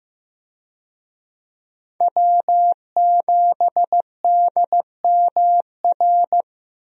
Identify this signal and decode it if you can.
Morse code, heard as a keyed tone, about 15 wpm, 700 Hz. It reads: W7DMR
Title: USA Call Signs - 5 Characters - Repeated 3x - 15wpm